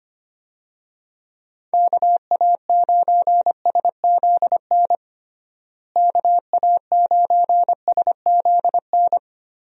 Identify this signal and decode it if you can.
Morse code, heard as a keyed tone, about 25 wpm, 700 Hz. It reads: KA9SZN KA9SZN
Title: USA Call Signs - 6 Characters - Repeated 2x - 25wpm